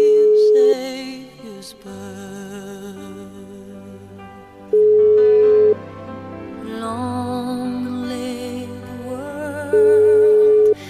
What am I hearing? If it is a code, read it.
Ringback tone (Europe, 425 Hz).